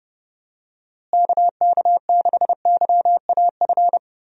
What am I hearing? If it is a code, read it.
Morse code, heard as a keyed tone, about 32 wpm, 700 Hz. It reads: KK6YAF
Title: USA Call Signs - 6 Characters - Repeated 3x - 30wpm